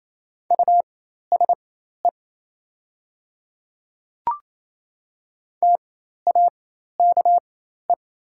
Morse code, heard as a keyed tone, about 28 wpm, spaced as for 13 wpm, 690 Hz. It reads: USE TAKE
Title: 4x Character Spacing - Top 200 Words - 28wpm